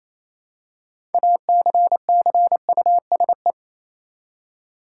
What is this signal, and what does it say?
Morse code, heard as a keyed tone, about 28 wpm, 700 Hz. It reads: ACCUSE
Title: Six Letter Words - 28wpm